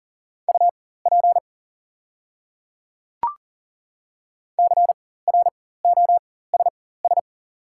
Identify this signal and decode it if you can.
Morse code, heard as a keyed tone, about 40 wpm, spaced as for 19 wpm, 700 Hz. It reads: UP CROSS
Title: 4x Character Spacing - Top 500 Words - 40wpm